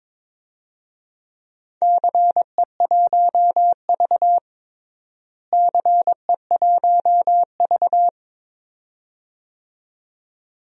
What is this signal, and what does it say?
Morse code, heard as a keyed tone, about 22 wpm, 700 Hz. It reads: CE1V CE1V
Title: DXCC-100 Call Signs - 4 Characters - Repeated 2x - 22wpm